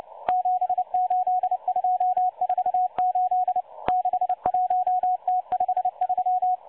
Morse code, heard as a keyed tone, about 29 wpm, 730 Hz. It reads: 78248 61T53
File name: M12-Morse-Numbers-Station-8116-kHz-CW